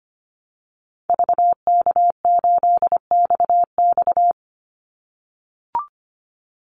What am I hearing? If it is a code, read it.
Morse code, heard as a keyed tone, about 25 wpm, 700 Hz. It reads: VK8XX
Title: Call Signs - 25wpm